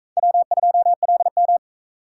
Morse code, heard as a keyed tone, about 45 wpm, 700 Hz. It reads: W2LM